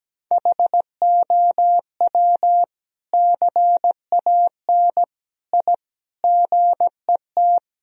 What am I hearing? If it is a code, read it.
Morse code, heard as a keyed tone, about 17 wpm, 700 Hz. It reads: HOW CAN I GET
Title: Sentences from Top 1000 Words - Review Part 3 - 17wpm